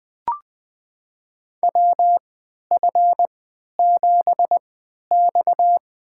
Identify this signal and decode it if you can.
Morse code, heard as a keyed tone, about 20 wpm, spaced as for 11 wpm, 700 Hz. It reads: WF7X
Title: USA Call Signs - 3x Character Spacing - 20wpm